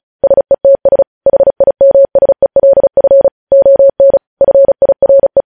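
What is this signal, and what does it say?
Morse code, heard as a keyed tone, about 36 wpm, 550 Hz. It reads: SETS HIMSELF ON FIRE
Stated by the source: Morse code news headlines practice sent at 35 wpm.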